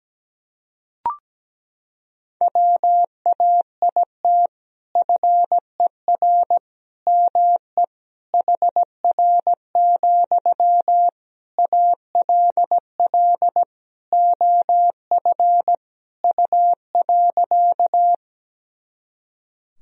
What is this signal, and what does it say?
Morse code, heard as a keyed tone, about 17 wpm, 700 Hz. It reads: WAIT FER ME HR, ALL OF U.
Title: Sentences from Top 500 Words and Abbreviations 17wpm